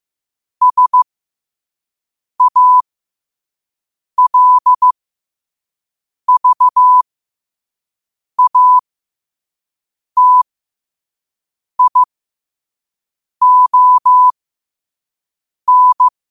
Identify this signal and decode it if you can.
Morse code, heard as a keyed tone, about 15 wpm, spaced as for 5 wpm, 1000 Hz. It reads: SALVATION
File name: qotdfiles / 5wpm